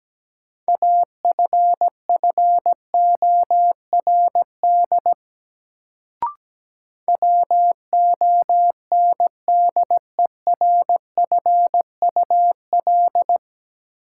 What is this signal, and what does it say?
Morse code, heard as a keyed tone, about 17 wpm, 700 Hz. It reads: AFFORD WONDERFUL
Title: Sentences from Top 1100 Words - New Word Drill Alt - 17wpm